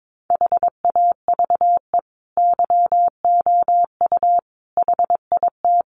Morse code, heard as a keyed tone, about 22 wpm, 700 Hz. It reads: HAVE YOU HIT
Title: Sentences from Top 1200 Words 22wpm